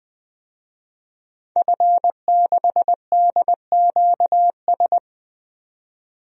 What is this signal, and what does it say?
Morse code, heard as a keyed tone, about 20 wpm, 700 Hz. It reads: F6DQS
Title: DXCC-100 Call Signs - 5 Characters - 20wpm